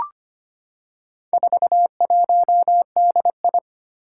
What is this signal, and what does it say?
Morse code, heard as a keyed tone, about 25 wpm, 700 Hz. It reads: 41DI
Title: Four Challenging Letter-Number Combinations 25wpm